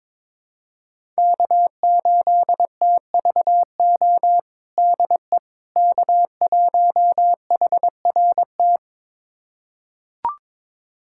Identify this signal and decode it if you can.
Morse code, heard as a keyed tone, about 22 wpm, 700 Hz. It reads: K8TVO DE K1HRT
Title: USA Callsign DE Callsign - 5 Characters - 22wpm